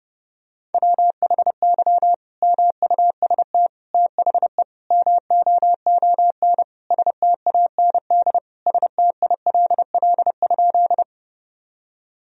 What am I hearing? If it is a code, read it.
Morse code, heard as a keyed tone, about 32 wpm, 700 Hz. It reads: WHY MUST THE MOON STAND STILL?
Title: Sentences from Top 1000 Words - Review Part 3 - 30wpm